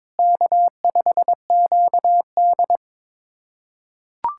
Morse code, heard as a keyed tone, about 22 wpm, 700 Hz. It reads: K5QD
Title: USA Callsign DE Callsign - 4 Characters - 22wpm